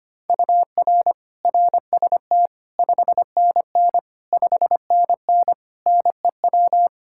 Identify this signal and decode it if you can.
Morse code, heard as a keyed tone, about 25 wpm, 700 Hz. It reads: UR RST 5NN 5NN NEW
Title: Practice QSO - Exchange 1 - Beginning - 25wpm